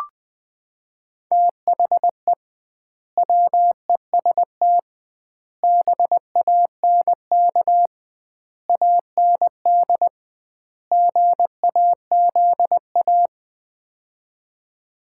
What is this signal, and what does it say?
Morse code, heard as a keyed tone, about 21 wpm, 700 Hz. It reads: THE WEST BANK AND GAZA
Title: 2x Word-Spacing - Sets of 5 Words - 20wpm